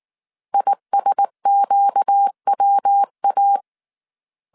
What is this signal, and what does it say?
Morse code, heard as a keyed tone, about 19 wpm, 800 Hz. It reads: ISQWA
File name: CW / DET / 1 6 1500C S C 18wpm 2356